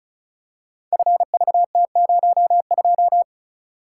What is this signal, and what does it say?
Morse code, heard as a keyed tone, about 34 wpm, 690 Hz. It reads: FVT02
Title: Code Groups - 5 Letter:Numbers - 35wpm